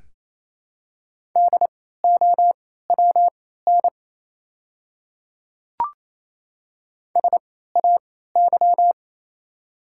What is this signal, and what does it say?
Morse code, heard as a keyed tone, about 28 wpm, spaced as for 16 wpm, 700 Hz. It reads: DOWN SAY
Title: 3x Character Spacing - Top 300 Words - 28wpm